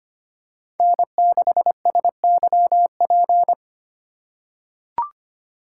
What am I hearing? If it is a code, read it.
Morse code, heard as a keyed tone, about 25 wpm, 700 Hz. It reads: N6SYP
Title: USA Call Signs - 5 Characters - 2x Character Spacing - 25wpm